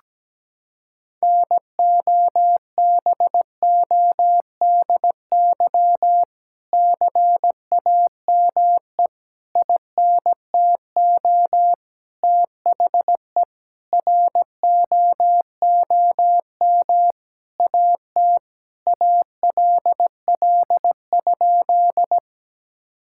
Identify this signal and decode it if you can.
Morse code, heard as a keyed tone, about 17 wpm, 700 Hz. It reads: NOBODY CAME INTO THE ROOM AT ALL?
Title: Sentences from Top 1600 Words 17wpm